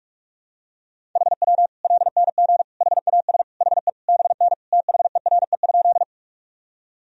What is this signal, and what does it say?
Morse code, heard as a keyed tone, about 46 wpm, 700 Hz. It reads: HW LNG HAS HE BN THERE?
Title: Sentences from Top 100 Words and Abbreviations 45wpm